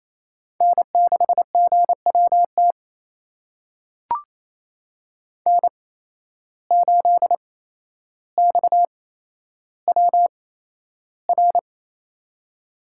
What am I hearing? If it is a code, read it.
Morse code, heard as a keyed tone, about 28 wpm, spaced as for 12 wpm, 700 Hz. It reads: N6GWT N8XWR
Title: USA Call Signs - 5 Characters - 8x Character Spacing - 28wpm